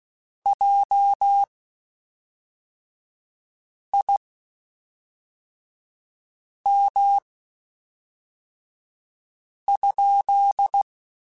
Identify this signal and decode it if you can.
Morse code, heard as a keyed tone, about 16 wpm, spaced as for 3 wpm, 800 Hz. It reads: JIM?
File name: QSOs / QSOs-mp3s / WD0BGP 1@5